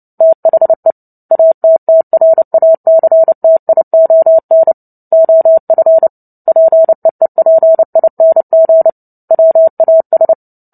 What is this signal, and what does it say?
Morse code, heard as a keyed tone, about 30 wpm, 650 Hz. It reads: THE ATTRACTION OF PEEPING WAS
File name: Book Two / 03-Chapter 3 - 29 WPM